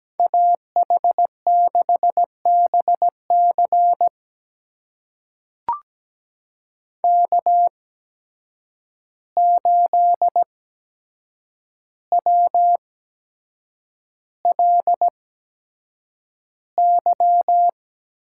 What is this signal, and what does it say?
Morse code, heard as a keyed tone, about 17 wpm, spaced as for 7 wpm, 700 Hz. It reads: AH6BC K8WLY
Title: USA Call Signs - 5 Characters - 8x Character Spacing - 17wpm